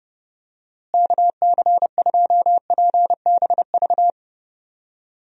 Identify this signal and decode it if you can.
Morse code, heard as a keyed tone, about 30 wpm, 700 Hz. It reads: KC2PBV
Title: USA Call Signs - 6 Characters - Repeated 3x - 30wpm